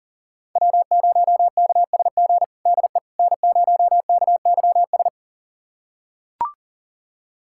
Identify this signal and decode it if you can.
Morse code, heard as a keyed tone, about 40 wpm, 700 Hz. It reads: W0KSG DE N0KYS
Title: USA Callsign DE Callsign - 5 Characters - 40wpm